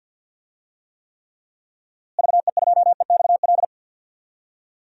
Morse code, heard as a keyed tone, about 49 wpm, 700 Hz. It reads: VE2EXL
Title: DXCC-100 Call Signs - 6 Characters - Repeated 3x - 50wpm